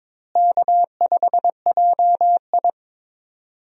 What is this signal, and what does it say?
Morse code, heard as a keyed tone, about 22 wpm, 700 Hz. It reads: K5JI
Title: USA Call Signs - 4 Characters - Repeated 3x - 22wpm